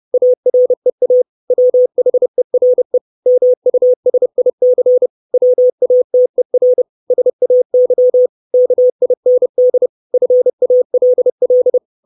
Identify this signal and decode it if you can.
Morse code, heard as a keyed tone, about 30 wpm, 500 Hz. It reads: AREA WHERE MUSIC WATER SAY KIND FALL
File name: 632 Words 30WPM